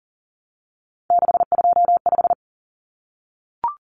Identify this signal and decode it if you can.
Morse code, heard as a keyed tone, about 40 wpm, 700 Hz. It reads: 625
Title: Numbers - 3-Digit 40wpm